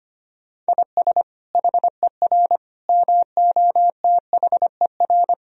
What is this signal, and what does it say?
Morse code, heard as a keyed tone, about 25 wpm, 700 Hz. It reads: IS HER MOTHER
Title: Sentences from Top 1000 Words - Review Part 4 - 25wpm